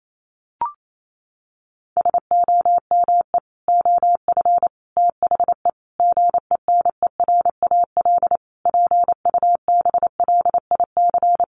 Morse code, heard as a keyed tone, about 28 wpm, 700 Hz. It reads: SOME OF THE GENERAL PUBLIC
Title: Sentences from Top 800 Words 28wpm